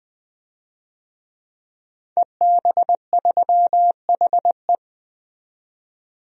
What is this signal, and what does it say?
Morse code, heard as a keyed tone, about 20 wpm, 700 Hz. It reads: EB3HE
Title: DXCC-100 Call Signs - 5 Characters - 20wpm